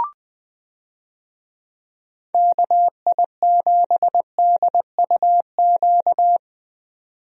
Morse code, heard as a keyed tone, about 19 wpm, 700 Hz. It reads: KI7DUQ
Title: USA Call Signs - 6 Characters - Repeated 3x - 20wpm